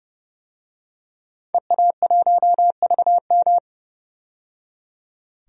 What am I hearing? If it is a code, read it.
Morse code, heard as a keyed tone, about 32 wpm, 700 Hz. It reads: EA1VM
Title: DXCC-100 Call Signs - 5 Characters - 30wpm